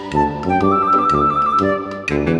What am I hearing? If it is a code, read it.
There is no code.